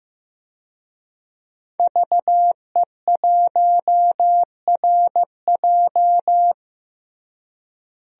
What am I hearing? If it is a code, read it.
Morse code, heard as a keyed tone, about 15 wpm, 700 Hz. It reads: VE1RJ
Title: DXCC-100 Call Signs - 5 Characters - Repeated 3x - 15wpm